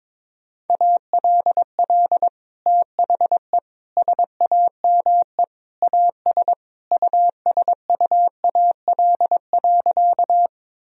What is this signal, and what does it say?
Morse code, heard as a keyed tone, about 21 wpm, 700 Hz. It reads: ALL THE SAME AS USUAL.
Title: Sentences from Top 1000 Words - Review Part 2 - 22wpm